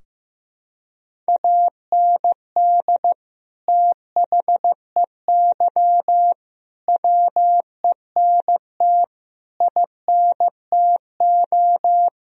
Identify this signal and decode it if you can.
Morse code, heard as a keyed tone, about 15 wpm, 700 Hz. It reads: AND THEY WENT INTO
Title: Sentences from Top 1000 Words - Review Part 2 - 15wpm